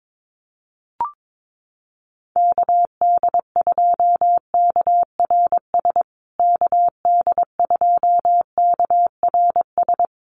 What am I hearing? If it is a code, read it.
Morse code, heard as a keyed tone, about 22 wpm, 700 Hz. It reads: KD2KRS KD2KRS
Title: USA Call Signs Repeated 2x 22wpm